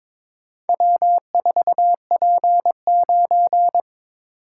Morse code, heard as a keyed tone, about 23 wpm, 690 Hz. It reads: W4P9
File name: practice / 22wpm / Four Letter-Number Combinations - 22wpm